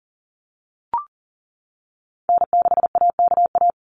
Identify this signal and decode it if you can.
Morse code, heard as a keyed tone, about 40 wpm, 700 Hz. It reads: N6AKA
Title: USA Call Signs Repeated 2x 40wpm